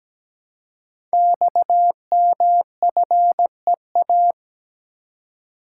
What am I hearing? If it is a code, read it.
Morse code, heard as a keyed tone, about 17 wpm, 700 Hz. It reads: XMFEA
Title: Code Groups - 5 Letter:Numbers - 17wpm